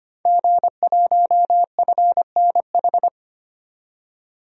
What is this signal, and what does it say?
Morse code, heard as a keyed tone, about 25 wpm, 700 Hz. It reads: G1FNH
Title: DXCC-100 Call Signs - 5 Characters - 25wpm